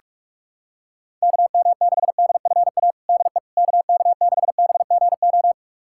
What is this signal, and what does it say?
Morse code, heard as a keyed tone, about 46 wpm, 700 Hz. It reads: KM6DFA DE KK6BGO
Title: USA Callsign DE Callsign - 6 Characters - 45wpm